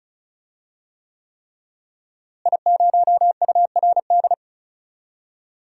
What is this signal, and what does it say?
Morse code, heard as a keyed tone, about 36 wpm, 700 Hz. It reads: I0URD
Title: DXCC-100 Call Signs - 5 Characters - Repeated 3x - 35wpm